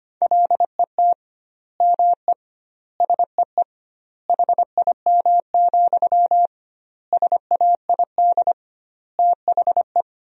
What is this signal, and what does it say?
Morse code, heard as a keyed tone, about 25 wpm, 700 Hz. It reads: LET ME SEE HIM, SAID THE
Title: 2x Word-Spacing - Sentences from Top 400 Words - 25wpm